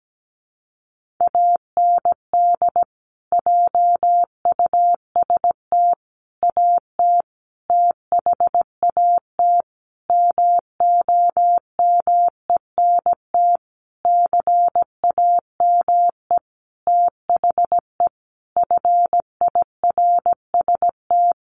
Morse code, heard as a keyed tone, about 17 wpm, 700 Hz. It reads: AND JUST AT THAT MOMENT CAME THE FIRST
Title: Sentences from Top 1500 Words 17wpm